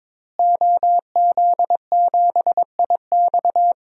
Morse code, heard as a keyed tone, about 22 wpm, 700 Hz. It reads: OZ7IX
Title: DXCC-100 Call Signs - 5 Characters - Repeated 3x - 22wpm